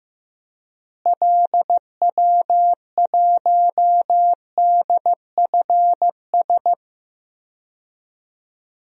Morse code, heard as a keyed tone, about 15 wpm, 700 Hz. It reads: LW1DFS
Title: DXCC-100 Call Signs - 6 Characters - Repeated 2x - 15wpm